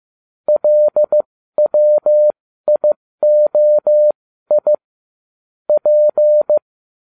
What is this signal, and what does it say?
Morse code, heard as a keyed tone, about 15 wpm, spaced as for 13 wpm, 600 Hz. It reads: LWIOI P
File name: cw / ubacw12 1